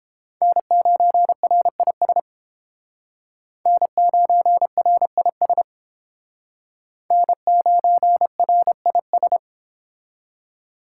Morse code, heard as a keyed tone, about 30 wpm, 700 Hz. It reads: N9RIS N9RIS N9RIS